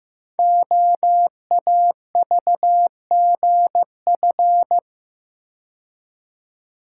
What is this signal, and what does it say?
Morse code, heard as a keyed tone, about 15 wpm, 700 Hz. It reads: OAVGF
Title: Code Groups - 5 Letters - 15wpm